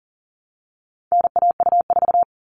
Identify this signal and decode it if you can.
Morse code, heard as a keyed tone, about 40 wpm, 700 Hz. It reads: NAU4
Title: All Cut Numbers - 4-Digit 40wpm